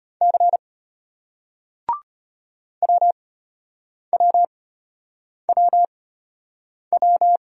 Morse code, heard as a keyed tone, about 34 wpm, spaced as for 8 wpm, 700 Hz. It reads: C WWWW